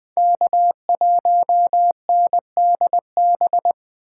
Morse code, heard as a keyed tone, about 20 wpm, 700 Hz. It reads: K1NDB
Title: USA Call Signs - 5 Characters - 4x Character Spacing - 20wpm